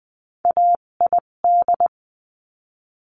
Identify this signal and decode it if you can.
Morse code, heard as a keyed tone, about 20 wpm, spaced as for 17 wpm, 700 Hz. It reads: AID
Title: 257 Lesson 42b - Words 14-20wpm